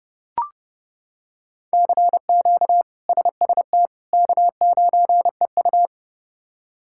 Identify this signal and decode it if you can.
Morse code, heard as a keyed tone, about 32 wpm, 700 Hz. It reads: CQ SST K9EU
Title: K1USN SST QSO Practice - 30wpm